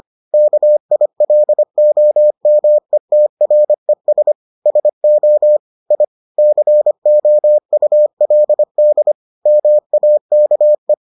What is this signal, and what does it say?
Morse code, heard as a keyed tone, about 25 wpm, 600 Hz. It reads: KILOMETRES SO I COULD MAKE